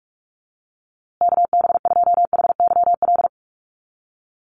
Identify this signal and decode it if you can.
Morse code, heard as a keyed tone, about 44 wpm, 700 Hz. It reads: KB2HYL
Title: USA Call Signs Repeated 2x 45wpm